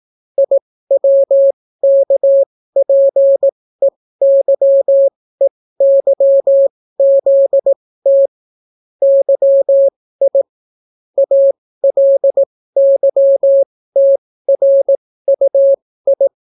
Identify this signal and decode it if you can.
Morse code, heard as a keyed tone, about 18 wpm, spaced as for 15 wpm, 550 Hz. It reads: IWKPEYEYZT YI ALYTRUI